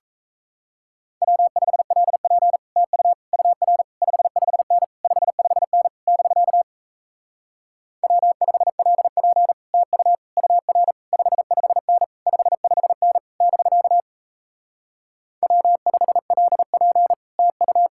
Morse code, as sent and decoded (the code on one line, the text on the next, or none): .-- ..... .-.. .--. / - ..- / ..- .-. / ..... ..... -. / ..... ..... -. / -...-.- / .-- ..... .-.. .--. / - ..- / ..- .-. / ..... ..... -. / ..... ..... -. / -...-.- / .-- ..... .-.. .--. / - ..-
W5LP TU UR 55N 55N <BK> W5LP TU UR 55N 55N <BK> W5LP TU